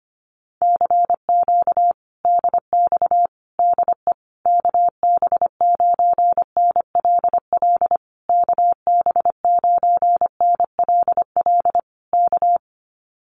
Morse code, heard as a keyed tone, about 25 wpm, 700 Hz. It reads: CQ DX DE KB9NLL KB9NLL K
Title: Common QSO Phrases Encore 25wpm